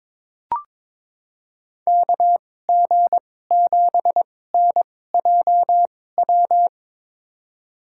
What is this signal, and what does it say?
Morse code, heard as a keyed tone, about 22 wpm, spaced as for 16 wpm, 700 Hz. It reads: KG7NJW
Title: USA Call Signs - 6 Characters - 2x Character Spacing - 22wpm